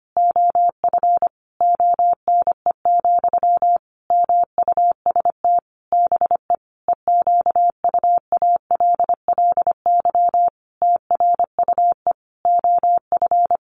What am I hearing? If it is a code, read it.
Morse code, heard as a keyed tone, about 25 wpm, 700 Hz. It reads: OF ONE, MUST BE EQUALLY TRUE OF
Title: Sentences from Top 1400 Words 25wpm